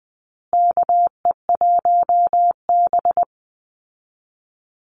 Morse code, heard as a keyed tone, about 20 wpm, 700 Hz. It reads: KE1B
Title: Call Signs Repeated 2x - 20wpm